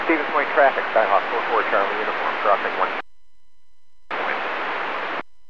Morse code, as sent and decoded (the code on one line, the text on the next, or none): none